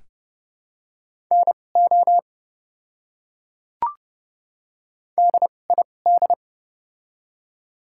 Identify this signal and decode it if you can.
Morse code, heard as a keyed tone, about 31 wpm, spaced as for 22 wpm, 700 Hz. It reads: NO DID
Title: 2x Character Spacing - Top 100 Words - 30wpm